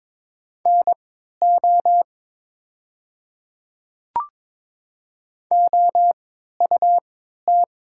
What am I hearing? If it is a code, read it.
Morse code, heard as a keyed tone, about 22 wpm, spaced as for 12 wpm, 700 Hz. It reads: NO OUT
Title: 3x Character Spacing - Top 200 Words - 22wpm